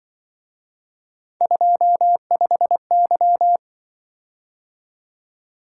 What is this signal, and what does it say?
Morse code, heard as a keyed tone, about 25 wpm, 700 Hz. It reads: 25Y
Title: Speed-Racing - Three Letter-Number Combinations - 20wpm